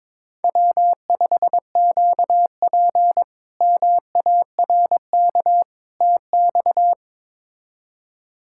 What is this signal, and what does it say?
Morse code, heard as a keyed tone, about 22 wpm, 700 Hz. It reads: W5QP MARK TX
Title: K1USN SST QSO Practice - 22wpm